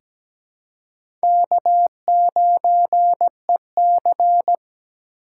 Morse code, heard as a keyed tone, about 17 wpm, 700 Hz. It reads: K9EC
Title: USA Call Signs - 3x Character Spacing - 17wpm